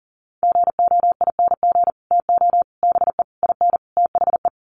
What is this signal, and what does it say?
Morse code, heard as a keyed tone, about 40 wpm, 700 Hz. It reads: GOING TO BE IN THE
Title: Sets of 5 Words Encore 40wpm